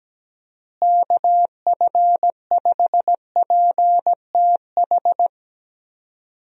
Morse code, heard as a keyed tone, about 17 wpm, 700 Hz. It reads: KF5PTH
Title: USA Call Signs - 6 Characters - Repeated 2x - 17wpm